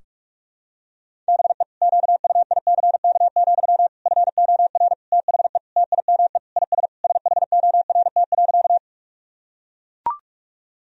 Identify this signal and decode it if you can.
Morse code, heard as a keyed tone, about 41 wpm, 700 Hz. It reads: BE QUICK, FOR THE TIME IS SHORT.
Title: Sentences from Top 1000 Words - Review Part 2 - 45wpm